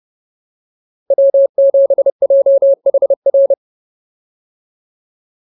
Morse code, heard as a keyed tone, about 30 wpm, 590 Hz. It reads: W7JHR